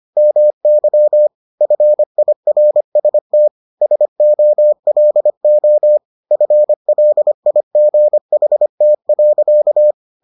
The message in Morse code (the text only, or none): MY FIRST SOLO FLIGHT.